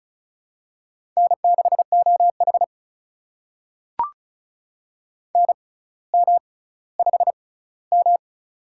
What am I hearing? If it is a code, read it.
Morse code, heard as a keyed tone, about 36 wpm, spaced as for 18 wpm, 700 Hz. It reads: N6OH NM5M
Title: USA Call Signs - 4 Characters - 6x Character Spacing - 35wpm